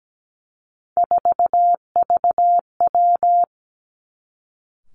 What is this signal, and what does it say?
Morse code, heard as a keyed tone, about 17 wpm, 700 Hz. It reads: 4VW
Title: Three Challenging Letter-Number Combinations 17wpm